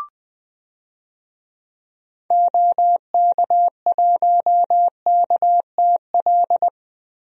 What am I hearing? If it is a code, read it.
Morse code, heard as a keyed tone, about 21 wpm, 700 Hz. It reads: OK1KTL
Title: DXCC-100 Call Signs - 6 Characters - Repeated 2x - 20wpm